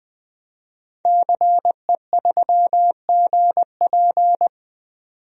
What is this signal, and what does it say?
Morse code, heard as a keyed tone, about 20 wpm, 700 Hz. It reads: CE3GP
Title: DXCC-100 Call Signs - 5 Characters - 20wpm